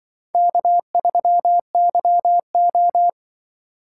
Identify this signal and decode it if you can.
Morse code, heard as a keyed tone, about 24 wpm, 700 Hz. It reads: K3YO
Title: Speed Racing - USA 1x2 Call Signs 20wpm